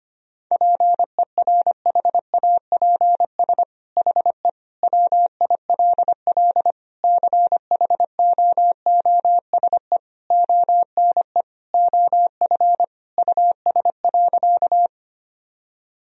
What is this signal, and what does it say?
Morse code, heard as a keyed tone, about 25 wpm, 700 Hz. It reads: PERHAPS HE WILL CHOOSE ONE OF US.
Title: Sentences from Top 1000 Words - Review Part 4 - 25wpm